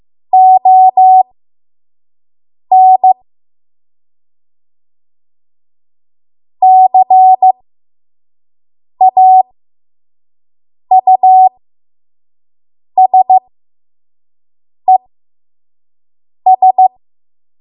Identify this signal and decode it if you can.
Morse code, heard as a keyed tone, about 15 wpm, spaced as for 5 wpm, 750 Hz. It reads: ON CAUSES
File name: Morse / 05WPM